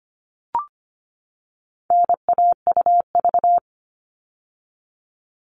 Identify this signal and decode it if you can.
Morse code, heard as a keyed tone, about 25 wpm, 700 Hz. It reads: NAUV
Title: All Cut Numbers - 4-Digit 25wpm